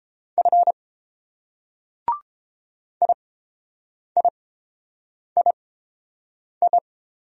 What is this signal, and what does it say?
Morse code, heard as a keyed tone, about 28 wpm, spaced as for 7 wpm, 700 Hz. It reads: F IIII